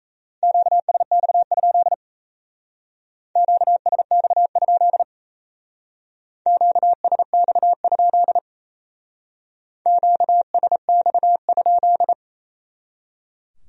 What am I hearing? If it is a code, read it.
Morse code, heard as a keyed tone, about 33 wpm, 700 Hz. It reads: QSX? QSX? QSX? QSX?
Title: Speed-Racing Q-Codes - 28wpm